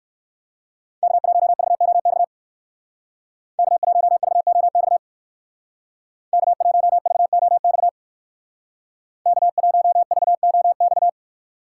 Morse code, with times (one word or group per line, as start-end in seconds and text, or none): 1.0-2.2 K1VOX
3.6-5.0 K1VOX
6.3-7.9 K1VOX
9.3-11.1 K1VOX